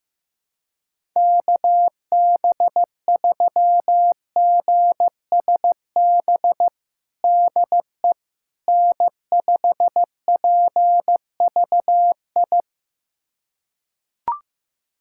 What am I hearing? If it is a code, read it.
Morse code, heard as a keyed tone, about 15 wpm, 700 Hz. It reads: KB3GSB DE N5PVI
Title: USA Callsign DE Callsign 15wpm